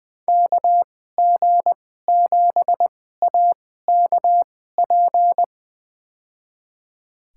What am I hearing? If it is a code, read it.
Morse code, heard as a keyed tone, about 21 wpm, spaced as for 15 wpm, 700 Hz. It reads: KG7AKP
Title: USA Call Signs - 6 Characters - 2x Character Spacing - 20wpm